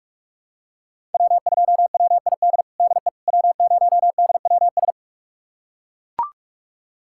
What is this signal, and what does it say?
Morse code, heard as a keyed tone, about 42 wpm, 700 Hz. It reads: W2WID DE W0DWS
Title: USA Callsign DE Callsign - 5 Characters - 45wpm